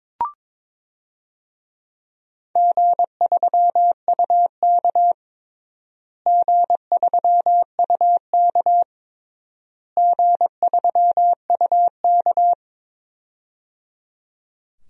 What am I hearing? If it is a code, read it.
Morse code, heard as a keyed tone, about 22 wpm, 700 Hz. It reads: G3UK G3UK G3UK
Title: DXCC-100 Call Signs - 4 Characters - Repeated 3x - 22wpm